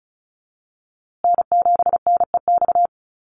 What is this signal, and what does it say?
Morse code, heard as a keyed tone, about 34 wpm, 700 Hz. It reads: N7NEX
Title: USA Call Signs Repeated 2x 35wpm